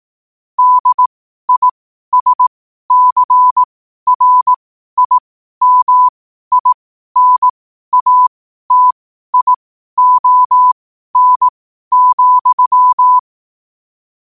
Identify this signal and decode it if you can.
Morse code, heard as a keyed tone, about 18 wpm, spaced as for 12 wpm, 1000 Hz. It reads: DISCRIMINATION,
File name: qotdfiles / 12wpm